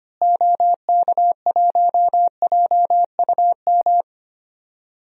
Morse code, heard as a keyed tone, about 25 wpm, 700 Hz. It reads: OK1JUM
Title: DXCC-100 Call Signs - 6 Characters - Repeated 2x - 25wpm